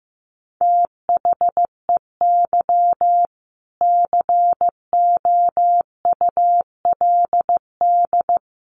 Morse code, heard as a keyed tone, about 15 wpm, 700 Hz. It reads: THEY COULD
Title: Sentences from Top 900 Words 15wpm